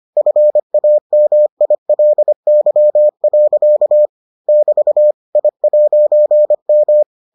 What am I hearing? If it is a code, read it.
Morse code, heard as a keyed tone, about 25 wpm, 600 Hz. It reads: FAMILY. <BT> I'M